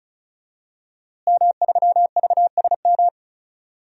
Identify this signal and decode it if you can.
Morse code, heard as a keyed tone, about 35 wpm, 700 Hz. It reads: M3VSM
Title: DXCC-100 Call Signs - 5 Characters - Repeated 3x - 35wpm